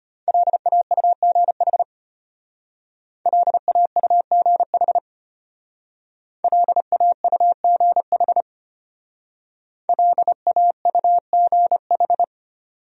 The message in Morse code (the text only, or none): LAUGH LAUGH LAUGH LAUGH